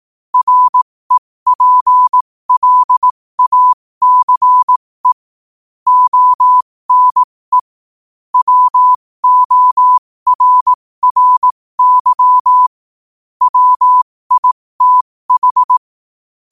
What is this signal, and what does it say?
Morse code, heard as a keyed tone, about 18 wpm, spaced as for 15 wpm, 1000 Hz. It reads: REPLACE ONE WORRY WITH
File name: qotdfiles / 15wpm